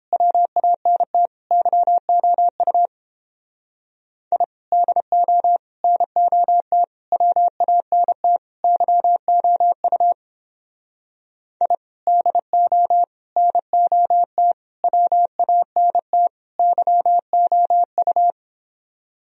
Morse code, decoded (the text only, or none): WANT YOU I DO NOT WANT YOU I DO NOT WANT YOU